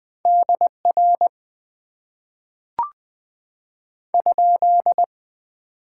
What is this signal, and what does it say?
Morse code, heard as a keyed tone, about 20 wpm, 700 Hz. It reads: DR ?
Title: Top 90 QSO Elements - 6x Character Spacing - 20wpm